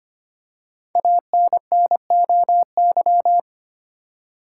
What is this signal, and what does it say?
Morse code, heard as a keyed tone, about 25 wpm, 700 Hz. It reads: ANNOY
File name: 5 Letter Words - First 3 Letters Given - 25wpm